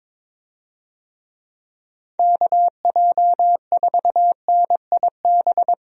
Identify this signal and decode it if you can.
Morse code, heard as a keyed tone, about 22 wpm, 700 Hz. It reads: KJ4NIB
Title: USA Call Signs - 6 Characters - Repeated 3x - 22wpm